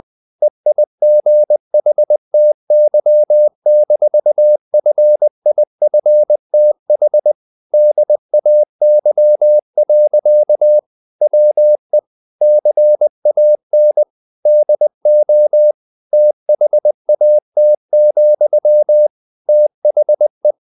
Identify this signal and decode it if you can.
Morse code, heard as a keyed tone, about 20 wpm, 600 Hz. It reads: EIGHTY-FIFTH DAY. WE CAN DO THAT, THE
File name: the old man and the sea / the old man and the sea part 1 20wpm